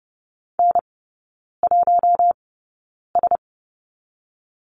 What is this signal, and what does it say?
Morse code, heard as a keyed tone, about 30 wpm, spaced as for 9 wpm, 700 Hz. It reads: N1S
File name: 081 Lesson 14e - Number 2 - Three Characters 30wpm